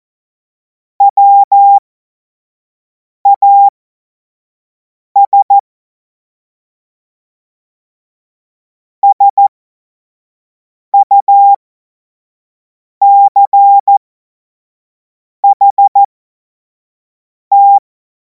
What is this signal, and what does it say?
Morse code, heard as a keyed tone, about 14 wpm, spaced as for 5 wpm, 800 Hz. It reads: WAS SUCHT
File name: Morsekurs / Witze / Witz 26. Mai 2026 25 F70